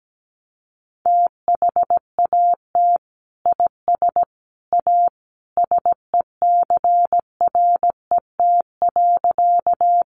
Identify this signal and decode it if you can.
Morse code, heard as a keyed tone, about 17 wpm, 700 Hz. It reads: THAT IS A SECRET.
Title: Sentences from Top 1200 Words 17wpm